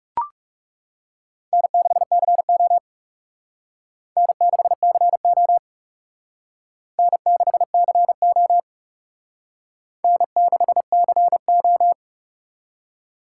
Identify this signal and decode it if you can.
Morse code, heard as a keyed tone, about 35 wpm, 700 Hz. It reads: N6CO N6CO N6CO N6CO
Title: Speed Racing - USA Call Signs 30wpm